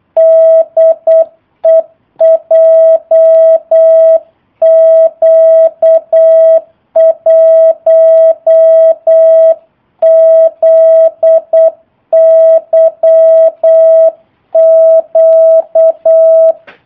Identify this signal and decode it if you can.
Morse code, heard as a keyed tone, about 8 wpm, 650 Hz. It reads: DE JQ1ZYQ
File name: jh8cbh / 2019cw / sp03- w